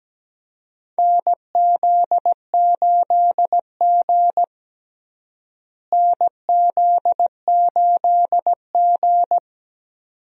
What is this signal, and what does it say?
Morse code, heard as a keyed tone, about 17 wpm, 700 Hz. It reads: NZ8G NZ8G
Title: USA Call Signs - 4 Characters - Repeated 2x - 17wpm